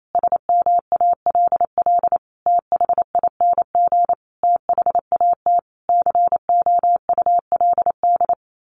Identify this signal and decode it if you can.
Morse code, heard as a keyed tone, about 28 wpm, 700 Hz. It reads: SMALL THING THAT COULD
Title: A Sound Of Thunder 28wpm